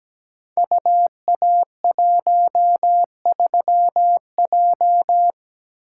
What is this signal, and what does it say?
Morse code, heard as a keyed tone, about 17 wpm, 700 Hz. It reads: UA13J
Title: Code Groups - 5 Letter:Numbers - 17wpm